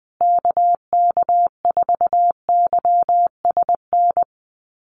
Morse code, heard as a keyed tone, about 19 wpm, 700 Hz. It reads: KK4YSN
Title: USA Call Signs Repeated 2x 20wpm